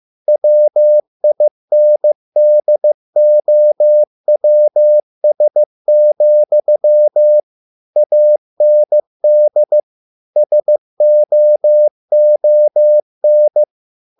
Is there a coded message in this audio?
Yes: Morse code from 0.3 to 13.6 s, heard as a keyed tone, about 15 wpm, 600 Hz. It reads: WINDOWS, AND SOON